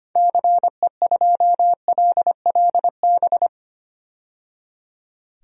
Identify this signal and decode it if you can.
Morse code, heard as a keyed tone, about 25 wpm, 700 Hz. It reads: CE2LLB
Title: DXCC-100 Call Signs - 6 Characters - 25wpm